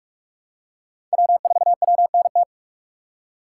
Morse code, heard as a keyed tone, about 44 wpm, 700 Hz. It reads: W4WNT
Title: K1USN SST QSO Practice - 45wpm